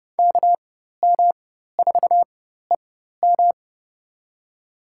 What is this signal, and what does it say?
Morse code, heard as a keyed tone, about 30 wpm, spaced as for 14 wpm, 700 Hz. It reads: KM4EM
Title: USA Call Signs - 5 Characters - 4x Character Spacing - 30wpm